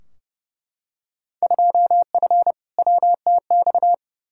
Morse code, heard as a keyed tone, about 30 wpm, 700 Hz. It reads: 2F WTX
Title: Field Day Exchanges - 30wpm